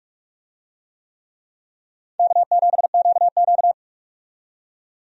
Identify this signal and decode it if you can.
Morse code, heard as a keyed tone, about 42 wpm, 700 Hz. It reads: K7QQ
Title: USA Call Signs - 4 Characters - 45wpm